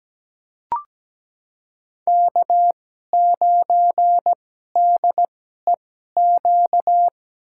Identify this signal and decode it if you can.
Morse code, heard as a keyed tone, about 17 wpm, spaced as for 12 wpm, 700 Hz. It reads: K9DEQ
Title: USA Call Signs - 5 Characters - 2x Character Spacing - 17wpm